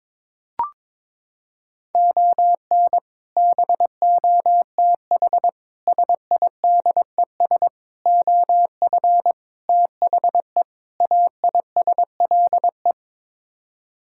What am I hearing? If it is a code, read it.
Morse code, heard as a keyed tone, about 22 wpm, 700 Hz. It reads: ON BOTH SIDES OF THE AISLE
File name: practice / 22wpm / Sets of 6 Words - 22wpm